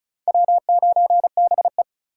Morse code, heard as a keyed tone, about 43 wpm, 700 Hz. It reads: W9BE
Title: Speed Racing - USA 1x2 Call Signs 35wpm